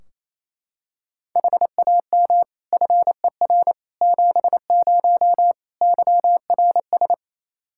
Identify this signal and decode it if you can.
Morse code, heard as a keyed tone, about 28 wpm, 700 Hz. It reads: HAM FER 70 YRS
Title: Practice QSO - Exchange 3 - More Info - 28wpm